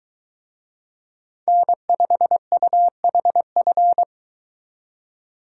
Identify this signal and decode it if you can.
Morse code, heard as a keyed tone, about 23 wpm, 700 Hz. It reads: N5UHF